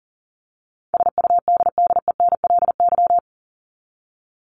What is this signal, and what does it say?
Morse code, heard as a keyed tone, about 40 wpm, 700 Hz. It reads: SUDDENLY
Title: Top 1501-1750 Words 40wpm